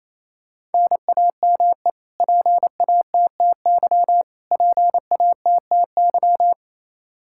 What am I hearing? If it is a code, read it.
Morse code, heard as a keyed tone, about 28 wpm, 700 Hz. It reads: NAME PATTY PATTY
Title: Practice QSO - Exchange 1 - Beginning - 28wpm